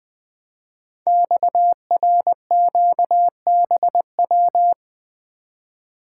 Morse code, heard as a keyed tone, about 20 wpm, 700 Hz. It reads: XRQBW
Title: Code Groups - 5 Letters - 20wpm